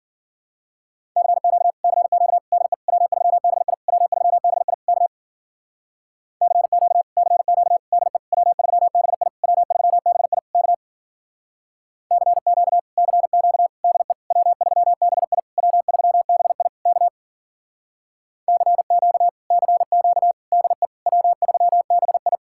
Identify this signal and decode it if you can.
Morse code, heard as a keyed tone, about 50 wpm, 700 Hz. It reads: CQ CQ DE W3BI W3BI K CQ CQ DE W3BI W3BI K CQ CQ DE W3BI W3BI K CQ CQ DE W3BI
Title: Speed-Racing Common QSO Phrases - 40wpm